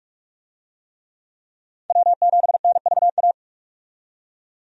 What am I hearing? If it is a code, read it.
Morse code, heard as a keyed tone, about 44 wpm, 700 Hz. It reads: W7NVA
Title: USA Call Signs - 5 Characters - 45wpm